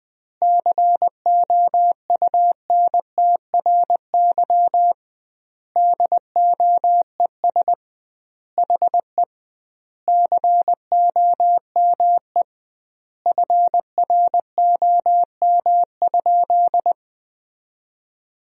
Morse code, heard as a keyed tone, about 20 wpm, 700 Hz. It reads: COUNTRY DOES HE COME FROM?
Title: 2x Word-Spacing - Sentences from Top 300 Words - 20wpm